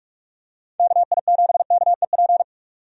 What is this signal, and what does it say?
Morse code, heard as a keyed tone, about 46 wpm, 700 Hz. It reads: KI7KEP
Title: USA Call Signs - 6 Characters - 45wpm